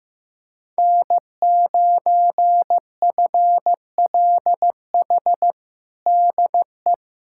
Morse code, heard as a keyed tone, about 15 wpm, 700 Hz. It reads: N9FLH DE
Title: USA Callsign DE Callsign 15wpm